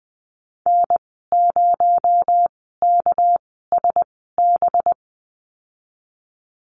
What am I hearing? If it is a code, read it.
Morse code, heard as a keyed tone, about 20 wpm, spaced as for 15 wpm, 700 Hz. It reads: N0KSB
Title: 238 Lesson 38g - Letter q - Call Signs 20wpm